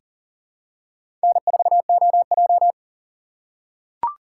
Morse code, heard as a keyed tone, about 40 wpm, 700 Hz. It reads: N4OJ
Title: USA Call Signs - 4 Characters - 4x Character Spacing - 40wpm